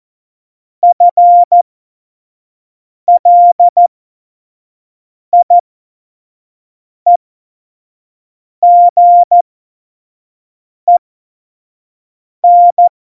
Morse code, heard as a keyed tone, about 14 wpm, spaced as for 5 wpm, 700 Hz. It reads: FLIEGEN
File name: Morsekurs / Witze / Witz 26. September 17, Tempo 25